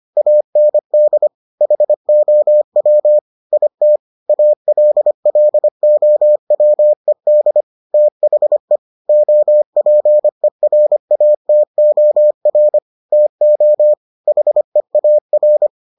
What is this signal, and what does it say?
Morse code, heard as a keyed tone, about 25 wpm, 600 Hz. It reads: AND HOW IT ALLOWED THE OPERATOR TO HEAR